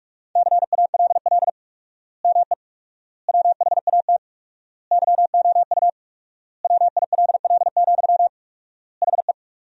Morse code, heard as a keyed tone, about 46 wpm, 700 Hz. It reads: CALL ME WHAT YOU WILL, HE
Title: 4x Word-Spacing - Sentences from Top 200 Words - 45wpm